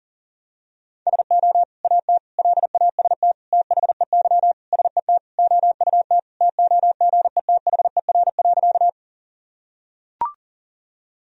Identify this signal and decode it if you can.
Morse code, heard as a keyed tone, about 40 wpm, 700 Hz. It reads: SO AT LAST THEY SET OUT TOGETHER.
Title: Sentences from Top 1000 Words - Review Part 1 - 40wpm